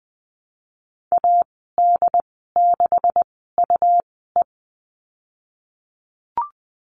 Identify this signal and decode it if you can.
Morse code, heard as a keyed tone, about 20 wpm, spaced as for 15 wpm, 700 Hz. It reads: AD6UE
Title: 128 Lesson 21g - Number 6 - Call Signs 20wpm